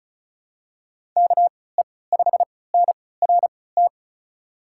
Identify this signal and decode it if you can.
Morse code, heard as a keyed tone, about 36 wpm, spaced as for 20 wpm, 700 Hz. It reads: KE5NRT
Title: USA Call Signs - 6 Characters - 3x Character Spacing - 35wpm